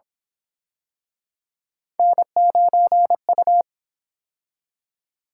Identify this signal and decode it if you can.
Morse code, heard as a keyed tone, about 28 wpm, 700 Hz. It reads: N9U